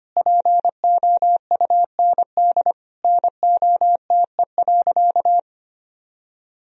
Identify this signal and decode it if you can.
Morse code, heard as a keyed tone, about 25 wpm, 700 Hz. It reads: POUND NOTE.
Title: Sentences from Top 1000 Words - Review Part 2 - 25wpm